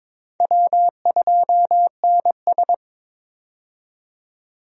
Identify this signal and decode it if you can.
Morse code, heard as a keyed tone, about 22 wpm, 700 Hz. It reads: W2NS
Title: USA Call Signs - 4 Characters - 22wpm